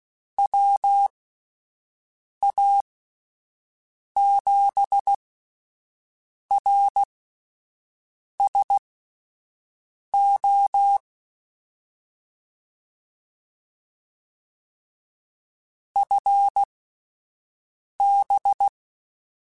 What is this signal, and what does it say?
Morse code, heard as a keyed tone, about 16 wpm, spaced as for 5 wpm, 800 Hz. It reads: WA7RSO FB
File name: (spaced) / QSOs Spaced-mp3s / WD0BGP 1 @10 sp